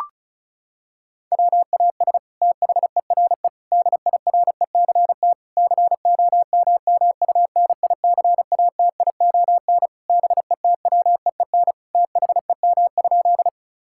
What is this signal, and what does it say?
Morse code, heard as a keyed tone, about 36 wpm, 700 Hz. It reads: WAS THERE DIRECT COMMUNICATION BETWEEN THEM?
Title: Sentences from Top 1000 Words - Review Part 6 - 35wpm